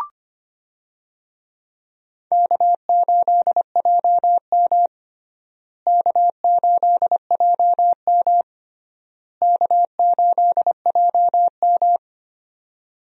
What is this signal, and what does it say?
Morse code, heard as a keyed tone, about 25 wpm, 700 Hz. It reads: K8JM K8JM K8JM
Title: USA Call Signs - 4 Characters - Repeated 3x - 25wpm